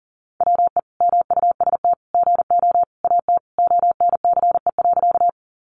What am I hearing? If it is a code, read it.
Morse code, heard as a keyed tone, about 40 wpm, 700 Hz. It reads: WE MUST GO AT ONCE.
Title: Sentences from Top 300 Words 40wpm